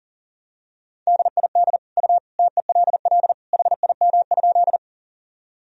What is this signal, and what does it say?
Morse code, heard as a keyed tone, about 37 wpm, 700 Hz. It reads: DID U TELL HIM?
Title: Sentences from Top 200 Words and Abbreviations 40wpm